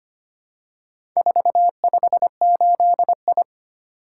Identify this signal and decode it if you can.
Morse code, heard as a keyed tone, about 25 wpm, 700 Hz. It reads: 458I
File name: practice / 25wpm / Four Letter-Number Combinations - 25wpm